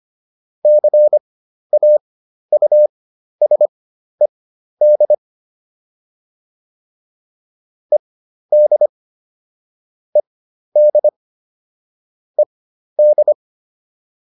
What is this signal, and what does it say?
Morse code, heard as a keyed tone, about 25 wpm, spaced as for 12 wpm, 600 Hz. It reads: CAUSED ED ED ED